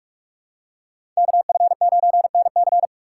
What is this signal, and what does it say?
Morse code, heard as a keyed tone, about 45 wpm, 700 Hz. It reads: KF9NC
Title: USA Call Signs - 5 Characters - 3x Character Spacing - 45wpm